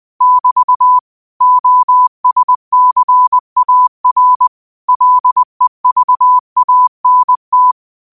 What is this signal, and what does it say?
Morse code, heard as a keyed tone, about 20 wpm, 1000 Hz. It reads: <BT> OSCAR LEVANT